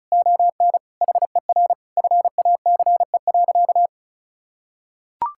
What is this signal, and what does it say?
Morse code, heard as a keyed tone, about 34 wpm, 700 Hz. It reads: ON HER FACE.
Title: Sentences from Top 1600 Words 35wpm